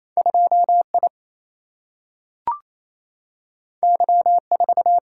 Morse code, heard as a keyed tone, about 28 wpm, 700 Hz. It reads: 2 I Y 4
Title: Single-Letter Single-Number 28wpm